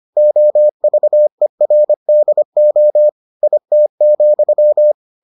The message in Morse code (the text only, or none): OVERDO IT,